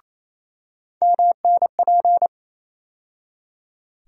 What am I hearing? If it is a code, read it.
Morse code, heard as a keyed tone, about 28 wpm, 700 Hz. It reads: MNP
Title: USA Call Sign Postfixes - 3 Letters 28wpm